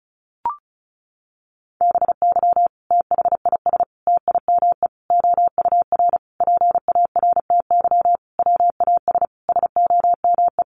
Morse code, heard as a keyed tone, about 34 wpm, 700 Hz. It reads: BY THIS TIME OUR PARTY WAS SOME
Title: Sentences from Top 1400 Words 35wpm